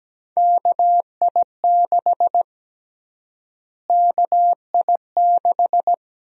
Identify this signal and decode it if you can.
Morse code, heard as a keyed tone, about 17 wpm, 700 Hz. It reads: KI6 KI6
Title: USA Call Sign Prefixes - Repeated 2x - 17wpm